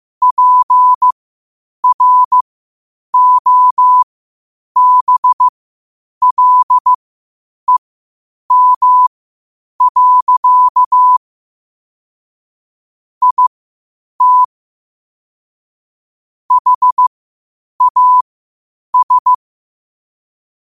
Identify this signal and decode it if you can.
Morse code, heard as a keyed tone, about 15 wpm, spaced as for 8 wpm, 1000 Hz. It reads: PROBLEM. IT HAS
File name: qotdfiles / 8wpm